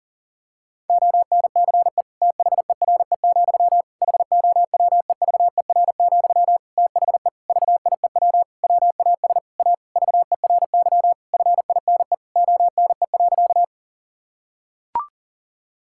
Morse code, heard as a keyed tone, about 40 wpm, 700 Hz. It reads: ONCE THERE, HOWEVER, THE VIEW WAS A VERY FINE ONE.
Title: Sentences from Top 1000 Words - Review Part 6 - 40wpm